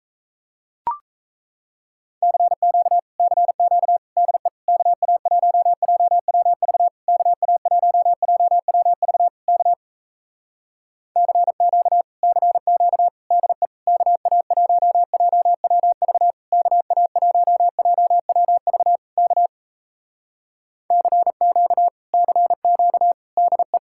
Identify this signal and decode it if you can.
Morse code, heard as a keyed tone, about 38 wpm, 700 Hz. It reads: CQ CQ DE KA1JWV KA1JWV K CQ CQ DE KA1JWV KA1JWV K CQ CQ DE